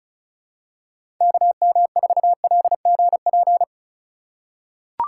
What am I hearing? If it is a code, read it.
Morse code, heard as a keyed tone, about 36 wpm, 700 Hz. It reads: KM4LGP
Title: USA Call Signs - 6 Characters - 3x Character Spacing - 35wpm